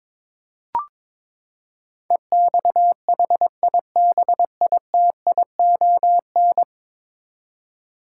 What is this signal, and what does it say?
Morse code, heard as a keyed tone, about 22 wpm, 700 Hz. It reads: EXHIBITION
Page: morse-code-ninja-practice-22wpm